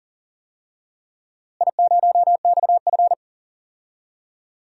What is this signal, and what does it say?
Morse code, heard as a keyed tone, about 40 wpm, 700 Hz. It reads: I0XF
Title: DXCC-100 Call Signs - 4 Characters - Repeated 2x - 40wpm